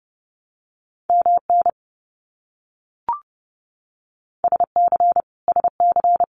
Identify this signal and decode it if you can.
Morse code, heard as a keyed tone, about 30 wpm, 700 Hz. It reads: MN SC SC
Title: US States Repeated 3x 30wpm